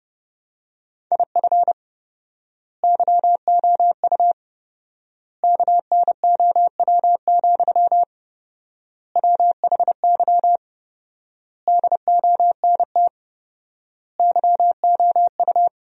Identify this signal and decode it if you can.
Morse code, heard as a keyed tone, about 32 wpm, 700 Hz. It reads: IF YOU KNOW, WHY DONT YOU
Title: 4x Word-Spacing - Sentences from Top 300 Words - 30wpm